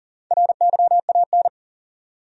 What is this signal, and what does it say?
Morse code, heard as a keyed tone, about 40 wpm, 700 Hz. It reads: RYAN
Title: USA Common Names - Max 4 Letters - 40wpm